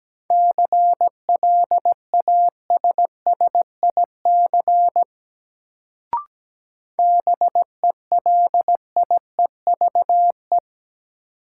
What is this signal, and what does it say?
Morse code, heard as a keyed tone, about 17 wpm, 700 Hz. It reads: CLASSIC BELIEVE
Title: Sentences from Top 1000 Words - New Word Drill - 17wpm